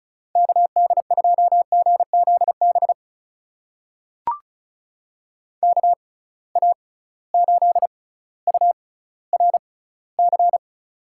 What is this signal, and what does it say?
Morse code, heard as a keyed tone, about 36 wpm, spaced as for 18 wpm, 700 Hz. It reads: KD2GZB KA8URC
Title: USA Call Signs - 6 Characters - 6x Character Spacing - 35wpm